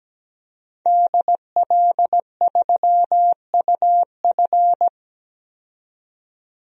Morse code, heard as a keyed tone, about 17 wpm, 700 Hz. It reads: DL3UF
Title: DXCC-100 Call Signs - 5 Characters - 17wpm